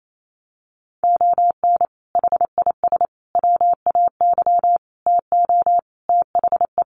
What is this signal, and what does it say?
Morse code, heard as a keyed tone, about 28 wpm, 700 Hz. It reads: ON HIS WAY TO THE
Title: Sets of 5 Words Encore2 28wpm